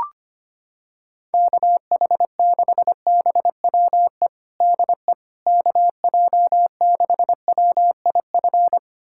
Morse code, heard as a keyed tone, about 25 wpm, 700 Hz. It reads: KH6BWE DE KJ6WIF
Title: USA Callsign DE Callsign 25wpm